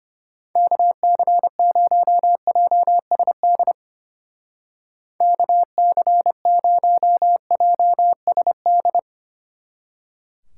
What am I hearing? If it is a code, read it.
Morse code, heard as a keyed tone, about 31 wpm, 700 Hz. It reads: KC0JSD KC0JSD
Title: Speed Racing - USA Call Signs 25wpm